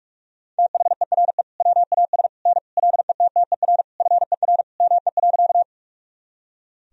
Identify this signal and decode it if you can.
Morse code, heard as a keyed tone, about 46 wpm, 700 Hz. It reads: THERE WAS N LETTER FER ME.
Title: Sentences from Top 300 Words and Abbreviations 45wpm